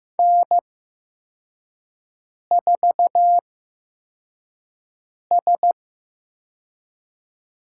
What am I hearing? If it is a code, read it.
Morse code, heard as a keyed tone, about 15 wpm, spaced as for 4 wpm, 700 Hz. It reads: N4S
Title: USA Call Signs - 5 Characters - 8x Character Spacing - 15wpm